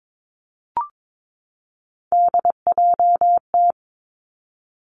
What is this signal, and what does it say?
Morse code, heard as a keyed tone, about 22 wpm, 700 Hz. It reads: DJT
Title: Three Challenging Letter-Number Combinations 22wpm